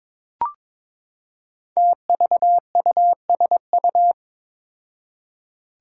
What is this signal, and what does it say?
Morse code, heard as a keyed tone, about 22 wpm, 700 Hz. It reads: TVUSU
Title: Five Challenging Letter-Number Combinations 22wpm